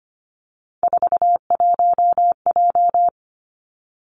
Morse code, heard as a keyed tone, about 25 wpm, 700 Hz. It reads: 41J
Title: Three Challenging Letter-Number Combinations 25wpm